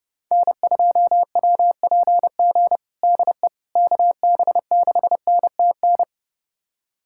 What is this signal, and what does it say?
Morse code, heard as a keyed tone, about 30 wpm, 700 Hz. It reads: N2WPG DE KB6NTN
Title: USA Callsign DE Callsign 30wpm